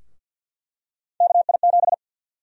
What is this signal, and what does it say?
Morse code, heard as a keyed tone, about 49 wpm, 700 Hz. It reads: KI7
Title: USA Call Sign Prefixes - 50wpm